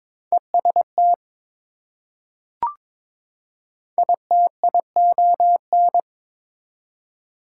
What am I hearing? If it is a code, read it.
Morse code, heard as a keyed tone, about 21 wpm, 700 Hz. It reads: EST ITION
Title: Prefixes and Suffixes 22wpm